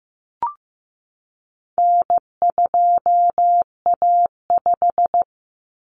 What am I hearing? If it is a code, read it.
Morse code, heard as a keyed tone, about 15 wpm, 700 Hz. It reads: N2A5
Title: Common Cut Numbers - 4-Digit 15wpm